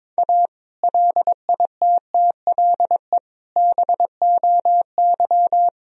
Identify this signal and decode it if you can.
Morse code, heard as a keyed tone, about 22 wpm, 700 Hz. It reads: A LITTLE BOY
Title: Sentences from Top 1000 Words - Review Part 1 - 22wpm